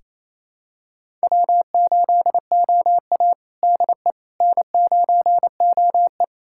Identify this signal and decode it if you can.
Morse code, heard as a keyed tone, about 28 wpm, 690 Hz. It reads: W8OA DE N9OE
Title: USA Callsign DE Callsign - 4 Characters - 28wpm